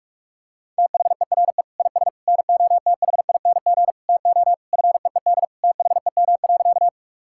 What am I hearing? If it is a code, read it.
Morse code, heard as a keyed tone, about 46 wpm, 700 Hz. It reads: THERE IS NOTHING TO FEED THEM.
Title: Sentences from Top 1600 Words 45wpm